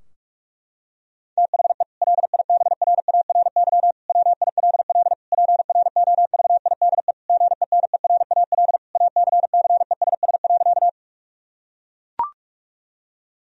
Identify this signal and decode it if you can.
Morse code, heard as a keyed tone, about 41 wpm, 700 Hz. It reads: THE LIBRARY WILL PROVIDE GENERAL ACCESS.
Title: Sentences from Top 1000 Words - Review Part 4 - 45wpm